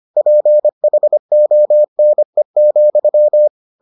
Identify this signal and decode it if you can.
Morse code, heard as a keyed tone, about 25 wpm, 600 Hz. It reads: PHONE,